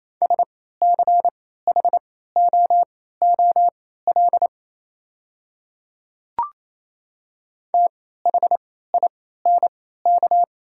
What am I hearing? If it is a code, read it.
Morse code, heard as a keyed tone, about 28 wpm, spaced as for 16 wpm, 700 Hz. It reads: SCHOOL THINK
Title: 3x Character Spacing - Top 500 Words - 28wpm